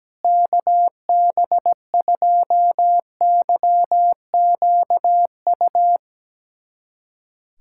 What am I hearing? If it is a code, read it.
Morse code, heard as a keyed tone, about 17 wpm, 700 Hz. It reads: KB2YQU
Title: USA Callsign DE Callsign 17wpm